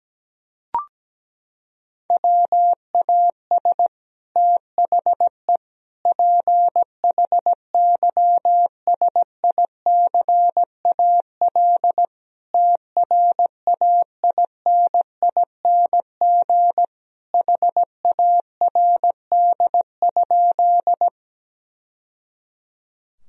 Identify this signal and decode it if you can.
Morse code, heard as a keyed tone, about 17 wpm, 700 Hz. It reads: WAS THE PHYSICAL TRAINING HARD?
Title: Sentences from Top 1000 Words - Review Part 4 - 17wpm